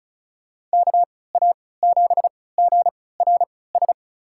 Morse code, heard as a keyed tone, about 36 wpm, spaced as for 20 wpm, 700 Hz. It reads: KA7GRS
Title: USA Call Signs - 6 Characters - 3x Character Spacing - 35wpm